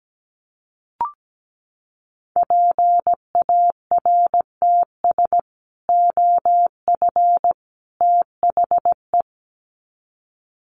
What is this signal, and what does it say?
Morse code, heard as a keyed tone, about 17 wpm, 700 Hz. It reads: PARTS OF THE
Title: Sets of 3 Words 17wpm